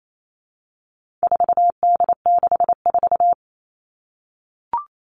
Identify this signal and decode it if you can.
Morse code, heard as a keyed tone, about 28 wpm, 700 Hz. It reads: 4D64
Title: All Cut Numbers - 4-Digit 28wpm